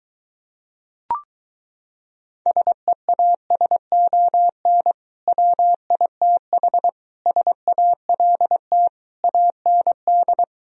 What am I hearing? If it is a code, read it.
Morse code, heard as a keyed tone, about 23 wpm, 700 Hz. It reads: SEASON WITH SALT AND